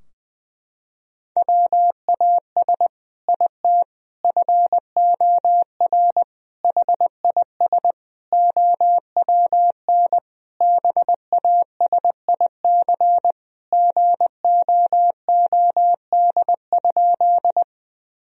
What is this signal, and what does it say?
Morse code, heard as a keyed tone, about 21 wpm, 700 Hz. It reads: WAS IT FOR HIS OWN BASIC GOOD?
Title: Sentences from Top 1000 Words - Review Part 3 - 20wpm